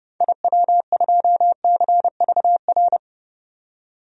Morse code, heard as a keyed tone, about 30 wpm, 700 Hz. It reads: IW2CVR
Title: DXCC-100 Call Signs - 6 Characters - 30wpm